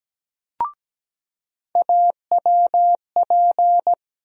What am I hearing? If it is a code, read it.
Morse code, heard as a keyed tone, about 17 wpm, 700 Hz. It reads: AWP
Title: USA Call Sign Postfixes - 3 Letters 17wpm